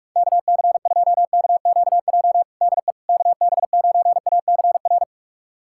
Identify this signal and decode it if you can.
Morse code, heard as a keyed tone, about 43 wpm, 700 Hz. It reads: KC2KQJ DE KB9ACR
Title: USA Callsign DE Callsign - 6 Characters - 45wpm